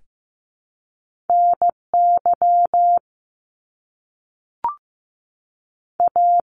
Morse code, heard as a keyed tone, about 15 wpm, 700 Hz. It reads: N Y A
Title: Top Two Letter Combinations in Words 15wpm